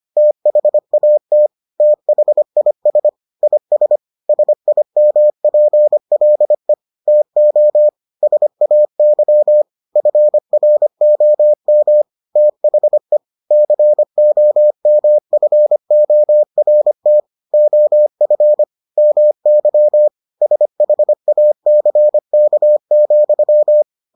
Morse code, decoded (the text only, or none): THAT THIS IS SIMPLE TO SAY FROM THE COMFORT OF MY SHACK,